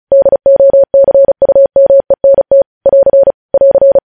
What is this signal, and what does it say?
Morse code, heard as a keyed tone, about 36 wpm, 550 Hz. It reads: DOCUMENT <AR> <AR>
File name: podcast / News headlines-1x-35x35